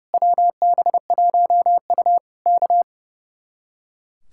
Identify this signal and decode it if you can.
Morse code, heard as a keyed tone, about 28 wpm, 700 Hz. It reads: WB1U K
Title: Speed-Racing Common QSO Phrases - 30wpm